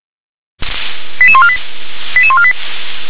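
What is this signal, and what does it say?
Five-tone selective call (ZVEI), heard as quick runs of five tones, tones of about 70 ms. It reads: 88148 88148
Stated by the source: ZVEI five-tone selective call received via SDR.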